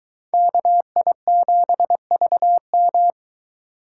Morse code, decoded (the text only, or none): KI7VM